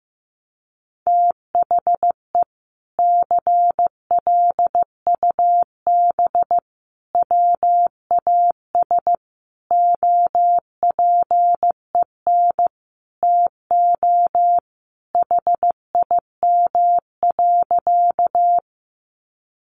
Morse code, heard as a keyed tone, about 15 wpm, 700 Hz. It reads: THE CLUB WAS OPEN TO HIM.
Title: Sentences from Top 1200 Words 15wpm